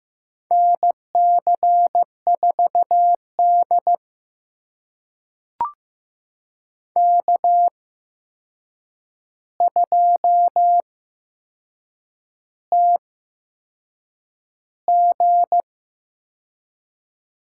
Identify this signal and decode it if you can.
Morse code, heard as a keyed tone, about 15 wpm, spaced as for 6 wpm, 700 Hz. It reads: NC4D K2TG
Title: USA Call Signs - 8x Character Spacing - 15wpm